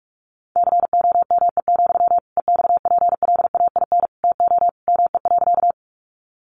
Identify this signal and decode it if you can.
Morse code, heard as a keyed tone, about 44 wpm, 700 Hz. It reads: COME, EXPLAIN TO ME.
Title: Sentences from Top 1100 Words 45wpm